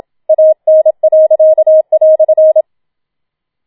Morse code, heard as a keyed tone, about 28 wpm, 630 Hz. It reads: AN."